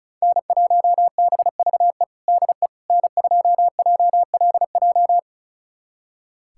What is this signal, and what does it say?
Morse code, heard as a keyed tone, about 34 wpm, 700 Hz. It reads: N1BVE DE N2JLJ
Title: USA Callsign DE Callsign - 5 Characters - 35wpm